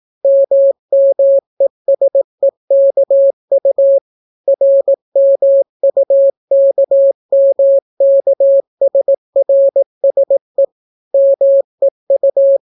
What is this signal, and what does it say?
Morse code, heard as a keyed tone, about 18 wpm, 550 Hz. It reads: MMESEKU RMUKMKSRSE MEU